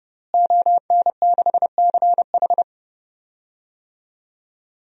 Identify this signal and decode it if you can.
Morse code, heard as a keyed tone, about 30 wpm, 700 Hz. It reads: ON6CH
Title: DXCC-100 Call Signs - 5 Characters - Repeated 3x - 30wpm